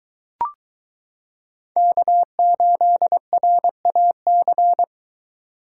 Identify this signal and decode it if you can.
Morse code, heard as a keyed tone, about 22 wpm, 700 Hz. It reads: K8RAC